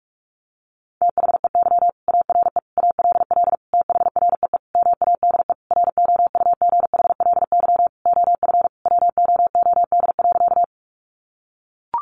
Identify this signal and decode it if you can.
Morse code, heard as a keyed tone, about 44 wpm, 700 Hz. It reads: THEY ARE ALL THREE MADE ROUGHLY OF WOOD.
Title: Sentences from Top 1500 Words 45wpm